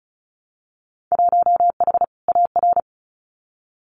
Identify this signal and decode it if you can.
Morse code, heard as a keyed tone, about 36 wpm, 700 Hz. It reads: 1H AR
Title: Winter Field Day Exchanges 35wpm